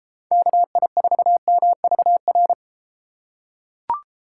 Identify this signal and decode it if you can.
Morse code, heard as a keyed tone, about 31 wpm, 700 Hz. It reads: KI4MVR